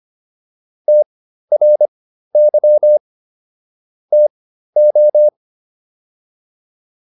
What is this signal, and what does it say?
Morse code, heard as a keyed tone, about 25 wpm, spaced as for 13 wpm, 600 Hz. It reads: TRY TO